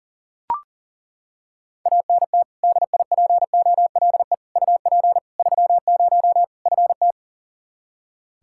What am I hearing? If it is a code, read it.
Morse code, heard as a keyed tone, about 37 wpm, 700 Hz. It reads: ANT DIPOLE UP 30 FT
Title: K1USN SST QSO Practice 40wpm